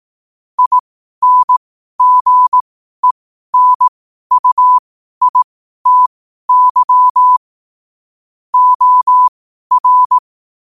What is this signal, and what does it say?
Morse code, heard as a keyed tone, about 18 wpm, spaced as for 12 wpm, 1000 Hz. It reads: INGENUITY OR
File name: qotdfiles / 12wpm